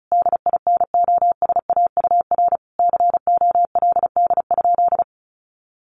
Morse code, heard as a keyed tone, about 34 wpm, 700 Hz. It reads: DINOSAUR COLD?
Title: A Sound Of Thunder 35wpm